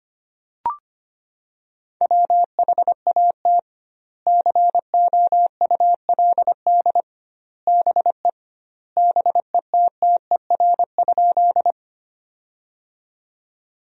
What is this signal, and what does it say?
Morse code, heard as a keyed tone, about 25 wpm, 700 Hz. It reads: WHAT COULD BE BETTER?
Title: 2x Word-Spacing - Sentences from Top 400 Words - 25wpm